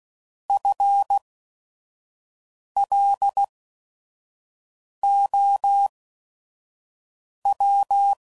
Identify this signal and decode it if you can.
Morse code, heard as a keyed tone, about 16 wpm, spaced as for 5 wpm, 800 Hz. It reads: FLOW
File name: Prefixes / By First Letter / Fs